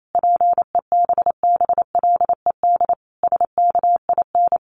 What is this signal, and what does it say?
Morse code, heard as a keyed tone, about 28 wpm, 700 Hz. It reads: PEBBLED SKIN
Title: A Sound Of Thunder 28wpm